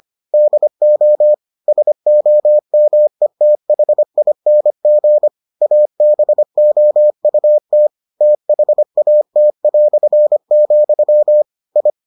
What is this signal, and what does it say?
Morse code, heard as a keyed tone, about 25 wpm, 600 Hz. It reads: DO SOMETHING ABOUT THAT", I